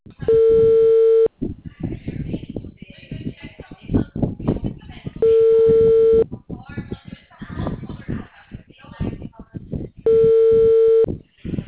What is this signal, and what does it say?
Ringback tone (China, 450 Hz), with speech.
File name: out- -2014- - - .51533